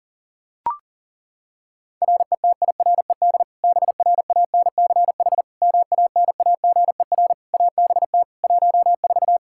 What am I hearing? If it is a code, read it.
Morse code, heard as a keyed tone, about 40 wpm, 700 Hz. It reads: RETIRED BRANCH MANAGER ABT 14
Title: Practice QSO - Exchange 3 - More Info - 40wpm